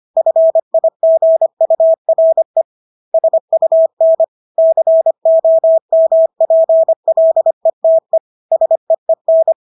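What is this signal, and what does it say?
Morse code, heard as a keyed tone, about 25 wpm, 650 Hz. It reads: FIGURE SUN COMPLETE SEEN
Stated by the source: Morse practice of English words at 25 wpm from CW med Gnister.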